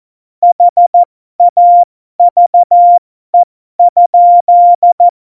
Morse code, heard as a keyed tone, about 14 wpm, spaced as for 12 wpm, 700 Hz. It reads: HAVE?
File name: Morsekurs / Witze